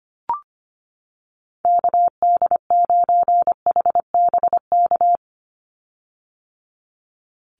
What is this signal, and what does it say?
Morse code, heard as a keyed tone, about 25 wpm, 700 Hz. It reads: KD9HBK
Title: USA Call Signs 25wpm